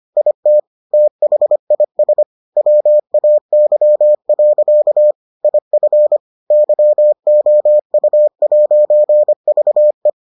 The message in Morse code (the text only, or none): IT THIS WAY. IF YOU'VE